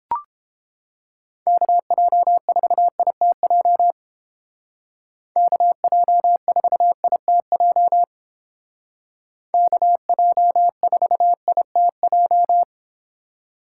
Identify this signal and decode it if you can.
Morse code, heard as a keyed tone, about 29 wpm, 700 Hz. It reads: KJ4ITJ KJ4ITJ KJ4ITJ